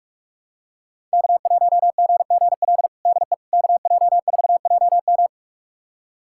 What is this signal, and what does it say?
Morse code, heard as a keyed tone, about 46 wpm, 700 Hz. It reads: K1GGL DE KJ4JM
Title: USA Callsign DE Callsign - 5 Characters - 45wpm